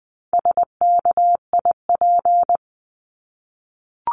Morse code, heard as a keyed tone, about 20 wpm, 700 Hz. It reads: SKIP
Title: 260 Lesson 42e - Words 20-20wpm